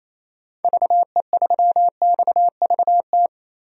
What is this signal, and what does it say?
Morse code, heard as a keyed tone, about 28 wpm, 700 Hz. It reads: VE3XVT
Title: DXCC-100 Call Signs - 6 Characters - 28wpm